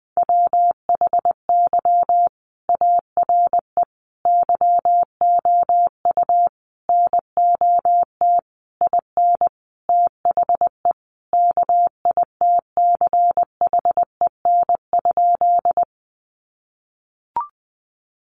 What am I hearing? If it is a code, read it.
Morse code, heard as a keyed tone, about 20 wpm, 700 Hz. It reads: WHY ARE YOU NOT IN THE KITCHEN?
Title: Sentences from Top 1400 Words 20wpm